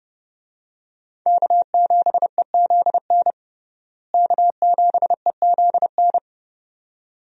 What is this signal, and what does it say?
Morse code, heard as a keyed tone, about 29 wpm, 700 Hz. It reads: K7EZN K7EZN
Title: USA Call Signs - 5 Characters - Repeated 2x - 30wpm